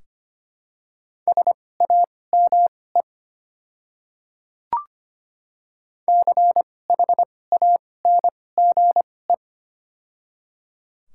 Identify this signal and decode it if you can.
Morse code, heard as a keyed tone, about 25 wpm, spaced as for 18 wpm, 700 Hz. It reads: SAME CHANGE
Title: 2x Character Spacing - Top 200 Words - 25wpm